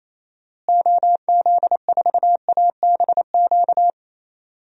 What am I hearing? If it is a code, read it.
Morse code, heard as a keyed tone, about 27 wpm, 700 Hz. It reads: OZ4ABQ
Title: DXCC-100 Call Signs - 6 Characters - Repeated 2x - 28wpm